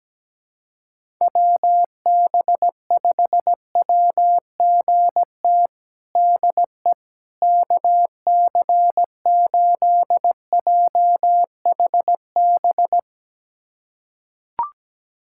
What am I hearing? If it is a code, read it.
Morse code, heard as a keyed tone, about 17 wpm, 700 Hz. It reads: WB5WGT DE KC8JHB
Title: USA Callsign DE Callsign 17wpm